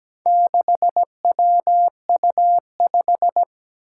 Morse code, heard as a keyed tone, about 17 wpm, 700 Hz. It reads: 6WU5
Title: Four Challenging Letter-Number Combinations 17wpm